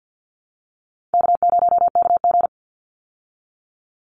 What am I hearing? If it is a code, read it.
Morse code, heard as a keyed tone, about 50 wpm, 700 Hz. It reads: K0KG
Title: Easy Call Signs Repeated 2x - 50wpm